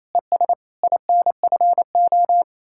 Morse code, heard as a keyed tone, about 34 wpm, 700 Hz. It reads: ES INFO
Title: Speed-Racing Common QSO Phrases - 28wpm